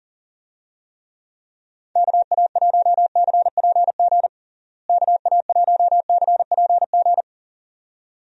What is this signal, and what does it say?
Morse code, heard as a keyed tone, about 40 wpm, 700 Hz. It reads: KA1CPG KA1CPG
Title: USA Call Signs - 6 Characters - Repeated 2x - 40wpm